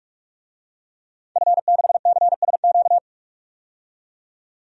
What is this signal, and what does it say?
Morse code, heard as a keyed tone, about 43 wpm, 700 Hz. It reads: F6CSQ
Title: DXCC-100 Call Signs - 5 Characters - 45wpm